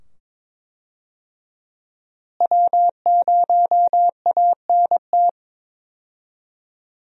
Morse code, heard as a keyed tone, about 22 wpm, 700 Hz. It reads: W0ANT
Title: USA Call Signs - 5 Characters - Repeated 2x - 22wpm